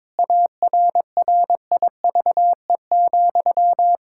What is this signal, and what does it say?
Morse code, heard as a keyed tone, about 22 wpm, 700 Hz. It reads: ARRIVE,
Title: Pangram Challenge v2 - 22wpm